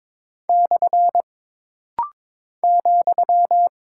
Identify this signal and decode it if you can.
Morse code, heard as a keyed tone, about 22 wpm, 700 Hz. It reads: / ,
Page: This rapid-fire timing is geared to push your ICR (Instant Character Recognition) skills.